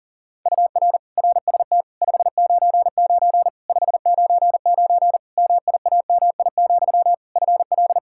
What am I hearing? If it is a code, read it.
Morse code, heard as a keyed tone, about 40 wpm, 700 Hz. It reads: UR RST 599 599 MIAMI, FL
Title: Practice QSO - Exchange 1 - Beginning - 40wpm